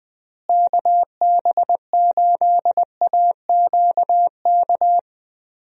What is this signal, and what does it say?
Morse code, heard as a keyed tone, about 20 wpm, 700 Hz. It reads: KB8AQK
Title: USA Call Signs - 6 Characters - Repeated 3x - 20wpm